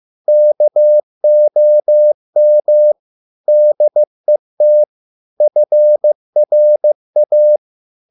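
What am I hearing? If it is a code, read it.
Morse code, heard as a keyed tone, about 15 wpm, 600 Hz. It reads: KOM DET FRA
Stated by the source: Morse code at a slow 15 wpm from CW med Gnister.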